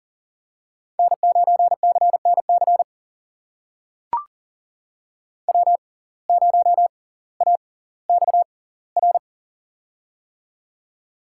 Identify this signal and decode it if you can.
Morse code, heard as a keyed tone, about 40 wpm, spaced as for 20 wpm, 700 Hz. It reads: N9CNC W0AXR
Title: USA Call Signs - 5 Characters - 6x Character Spacing - 40wpm